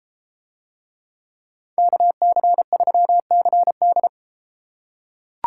Morse code, heard as a keyed tone, about 32 wpm, 700 Hz. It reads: KC3CD